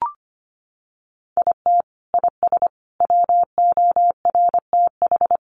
Morse code, heard as a keyed tone, about 25 wpm, 700 Hz. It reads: IT IS WORTH
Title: Sentences from Top 1400 Words 25wpm